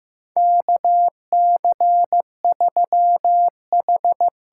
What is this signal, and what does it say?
Morse code, heard as a keyed tone, about 15 wpm, 700 Hz. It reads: KC3H
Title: USA Call Signs - 6 Characters - 8x Character Spacing - 15wpm